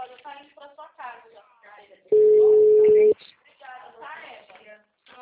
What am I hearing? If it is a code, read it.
Ringback tone (Europe, 425 Hz), with speech.
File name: out- -2024- - - .9886